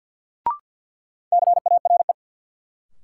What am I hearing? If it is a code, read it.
Morse code, heard as a keyed tone, about 51 wpm, 700 Hz. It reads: CARE
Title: Sets of 3 Words - New Word Drill - 50wpm